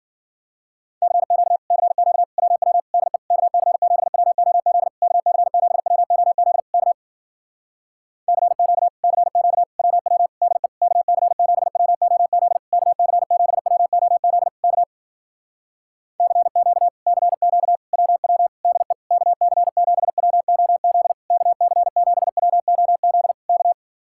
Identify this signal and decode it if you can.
Morse code, heard as a keyed tone, about 60 wpm, 700 Hz. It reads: CQ CQ WW DE KC7WOZ KC7WOZ K CQ CQ WW DE KC7WOZ KC7WOZ K CQ CQ WW DE KC7WOZ KC7WOZ K